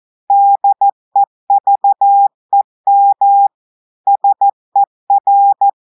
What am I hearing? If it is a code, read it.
Morse code, heard as a keyed tone, about 14 wpm, 800 Hz. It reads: DEVEM SER
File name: audios / portugais / portugais texte en clair 2 vitesse 14 mots minutes